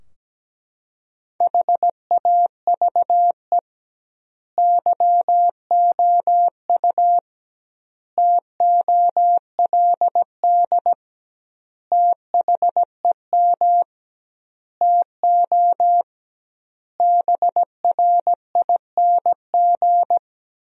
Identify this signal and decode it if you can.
Morse code, heard as a keyed tone, about 17 wpm, 700 Hz. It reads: HAVE YOU TOLD THEM TO BRING
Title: 2x Word-Spacing - Sentences from Top 500 Words - 17wpm